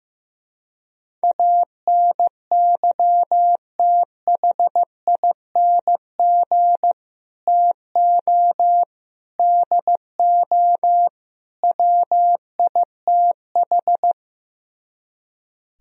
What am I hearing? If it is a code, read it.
Morse code, heard as a keyed tone, about 15 wpm, 700 Hz. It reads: ANYTHING TO DO WITH
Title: Speed-Racing - Sets of 4 Words 15wpm